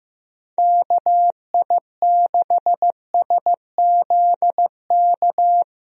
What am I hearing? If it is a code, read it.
Morse code, heard as a keyed tone, about 15 wpm, 700 Hz. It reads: KI6SZK
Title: USA Call Signs - 6 Characters - 15wpm